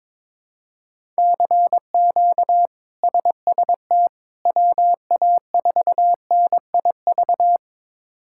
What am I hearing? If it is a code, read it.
Morse code, heard as a keyed tone, about 22 wpm, 700 Hz. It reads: CQ SST WA4NIV
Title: K1USN SST QSO Practice - 22wpm